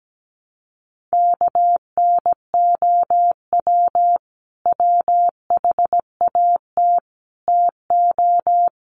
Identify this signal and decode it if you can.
Morse code, heard as a keyed tone, about 17 wpm, 700 Hz. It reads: KNOW WHAT TO
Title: Sets of 4 Words 17wpm